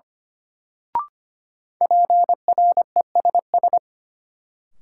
Morse code, heard as a keyed tone, about 25 wpm, 700 Hz. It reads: PRESS
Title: Sentences from Top 800 Words - New Word Drill - 25wpm